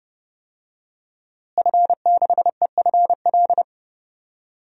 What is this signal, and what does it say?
Morse code, heard as a keyed tone, about 31 wpm, 700 Hz. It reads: F6EFL
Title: DXCC-100 Call Signs - 5 Characters - Repeated 2x - 30wpm